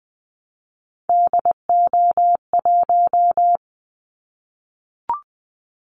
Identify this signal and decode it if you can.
Morse code, heard as a keyed tone, about 20 wpm, 700 Hz. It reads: DO1
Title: Callsign Prefixes for Top 25 DXCC - Repeated 3x - 20wpm